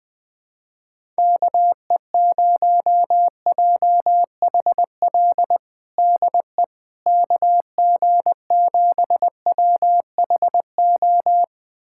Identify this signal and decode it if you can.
Morse code, heard as a keyed tone, about 20 wpm, 700 Hz. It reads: KE0JHL DE KG7WHO
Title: USA Callsign DE Callsign 20wpm